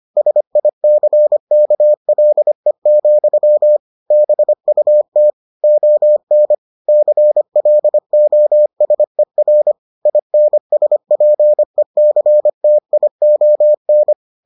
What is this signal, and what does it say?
Morse code, heard as a keyed tone, about 25 wpm, 600 Hz. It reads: SICKLE, BUT ON CLOSER INSPECTION